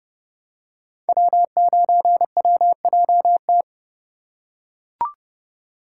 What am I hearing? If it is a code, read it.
Morse code, heard as a keyed tone, about 30 wpm, 700 Hz. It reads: W9WJT
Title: Code Groups - 5 Letter:Numbers - 30wpm